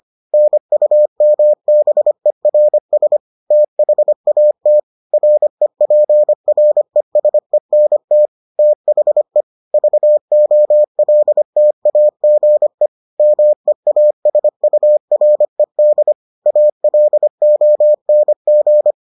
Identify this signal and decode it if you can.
Morse code, heard as a keyed tone, about 25 wpm, 600 Hz. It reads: NUMBERS THAT REPRESENT THE VOLTAGE MEASURED ALONG